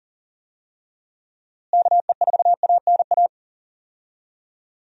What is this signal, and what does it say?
Morse code, heard as a keyed tone, about 40 wpm, 700 Hz. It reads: KE4ANA
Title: USA Call Signs - 6 Characters - Repeated 3x - 40wpm